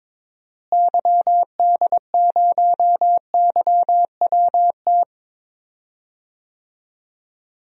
Morse code, heard as a keyed tone, about 22 wpm, 700 Hz. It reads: YD0YWT
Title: DXCC-100 Call Signs - 6 Characters - Repeated 3x - 22wpm